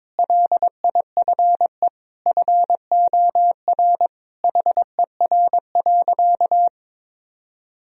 Morse code, heard as a keyed tone, about 22 wpm, 700 Hz. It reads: LIFE FOR HER.
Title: Sentences from Top 1000 Words - Review Part 2 - 22wpm